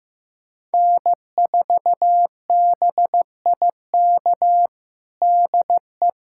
Morse code, heard as a keyed tone, about 15 wpm, 700 Hz. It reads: N4BIK DE
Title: USA Callsign DE Callsign - 5 Characters - 15wpm